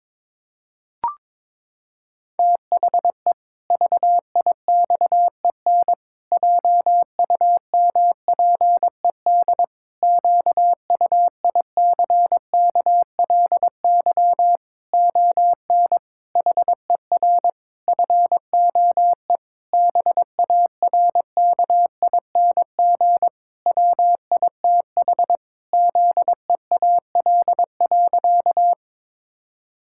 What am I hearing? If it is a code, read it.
Morse code, heard as a keyed tone, about 22 wpm, 700 Hz. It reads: THE VIXEN JUMPED QUICKLY ON HER FOE BARKING WITH ZEAL.
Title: Pangram Challenge v2 - 22wpm